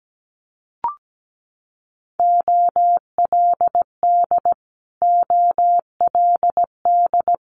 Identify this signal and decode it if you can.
Morse code, heard as a keyed tone, about 17 wpm, 700 Hz. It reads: OLD OLD
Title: Multi-letter Phonemes Repeated 3x 17wpm